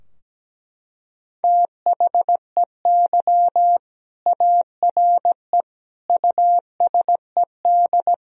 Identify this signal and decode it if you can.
Morse code, heard as a keyed tone, about 17 wpm, 700 Hz. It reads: THEY ARE USED
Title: Sentences from Top 1000 Words - Review Part 3 - 17wpm